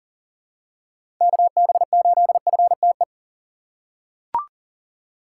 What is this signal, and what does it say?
Morse code, heard as a keyed tone, about 40 wpm, 700 Hz. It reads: KB8FTE
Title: USA Call Signs - 6 Characters - 2x Character Spacing - 40wpm